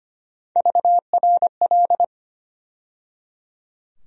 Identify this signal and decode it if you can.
Morse code, heard as a keyed tone, about 25 wpm, 700 Hz. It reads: VRL
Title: USA Call Sign Postfixes - 3 Letters 25wpm